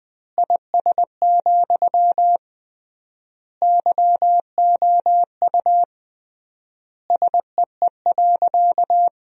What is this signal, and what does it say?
Morse code, heard as a keyed tone, about 21 wpm, 700 Hz. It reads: IS, YOU SEE.
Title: 3x Word-Spacing - Sentences from Top 100 Words - 20wpm